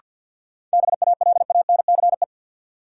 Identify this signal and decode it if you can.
Morse code, heard as a keyed tone, about 53 wpm, 700 Hz. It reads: BALANCE
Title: Sentences from Top 900 Words - New Word Drill - 50wpm